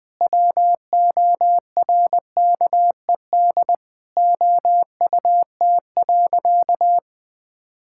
Morse code, heard as a keyed tone, about 20 wpm, 700 Hz. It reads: WORKED OUT.
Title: Sentences with 3-Word Phrases - 20wpm